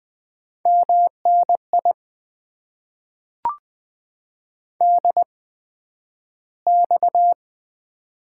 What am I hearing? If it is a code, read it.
Morse code, heard as a keyed tone, about 20 wpm, spaced as for 11 wpm, 700 Hz. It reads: MNI DX
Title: Top 90 QSO Elements - 8x Character Spacing - 20wpm